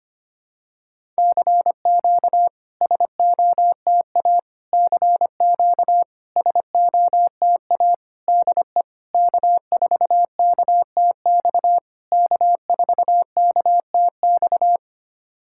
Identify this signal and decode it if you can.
Morse code, heard as a keyed tone, about 25 wpm, 700 Hz. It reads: CQ SOTA CQ SOTA DE K4KTX K4KTX
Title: SOTA QSO Practice 25wpm